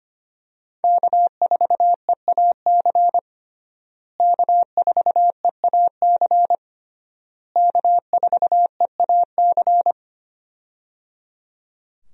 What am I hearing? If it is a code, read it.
Morse code, heard as a keyed tone, about 25 wpm, 700 Hz. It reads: K4EAC K4EAC K4EAC
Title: USA Call Signs - 5 Characters - Repeated 3x - 25wpm